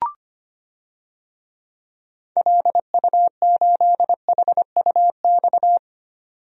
Morse code, heard as a keyed tone, about 25 wpm, 700 Hz. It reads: LU8HUX
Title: DXCC-100 Call Signs - 6 Characters - 25wpm